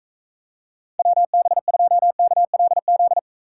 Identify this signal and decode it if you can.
Morse code, heard as a keyed tone, about 40 wpm, 700 Hz. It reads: WB2KLZ